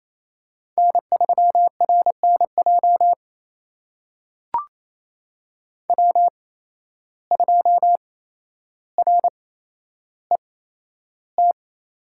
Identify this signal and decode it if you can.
Morse code, heard as a keyed tone, about 28 wpm, spaced as for 12 wpm, 700 Hz. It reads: N3RNJ W2RET
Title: USA Call Signs - 5 Characters - 8x Character Spacing - 28wpm